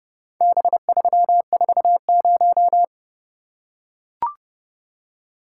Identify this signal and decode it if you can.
Morse code, heard as a keyed tone, about 30 wpm, 700 Hz. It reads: B340